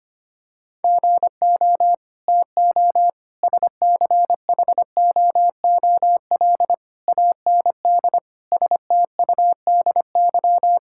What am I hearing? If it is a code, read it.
Morse code, heard as a keyed tone, about 25 wpm, 700 Hz. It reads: GO TO SCHOOL AND STUDY
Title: Sentences from Top 1000 Words - Review Part 6 - 25wpm